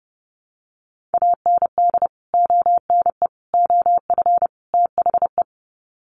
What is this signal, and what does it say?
Morse code, heard as a keyed tone, about 30 wpm, 700 Hz. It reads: AND ONE OF THE
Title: Sets of 4 Words 30wpm